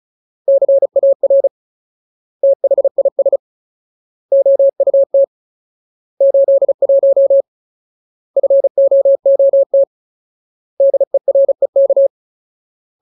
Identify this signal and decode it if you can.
Morse code, heard as a keyed tone, about 34 wpm, 550 Hz. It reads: CAR THIS OUT 81 FOOT DEREK
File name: Ultimate Word List / 35WPM Ultimate